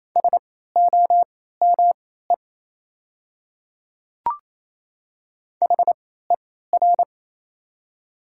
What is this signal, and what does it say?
Morse code, heard as a keyed tone, about 28 wpm, spaced as for 16 wpm, 700 Hz. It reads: SOME HER
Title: 3x Character Spacing - Top 100 Words - 28wpm